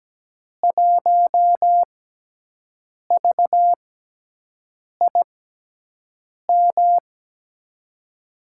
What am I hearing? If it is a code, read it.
Morse code, heard as a keyed tone, about 17 wpm, spaced as for 6 wpm, 700 Hz. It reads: 1VIM
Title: USA Call Signs - 5 Characters - 6x Character Spacing - 17wpm